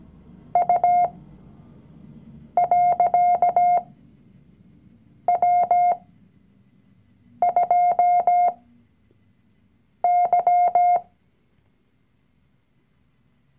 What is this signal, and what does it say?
Morse code, heard as a keyed tone, about 17 wpm, spaced as for 5 wpm, 710 Hz. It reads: U.W2Y
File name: CW-Online-Kurs / Anfänger / Lektion 019